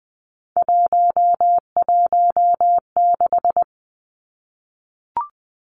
Morse code, heard as a keyed tone, about 20 wpm, 700 Hz. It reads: 116
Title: Numbers - 3-Digit 20wpm